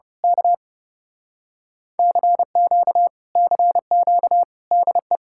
Morse code, heard as a keyed tone, about 28 wpm, 700 Hz. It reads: K CQ CQ DE
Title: Speed-Racing Common QSO Phrases - 30wpm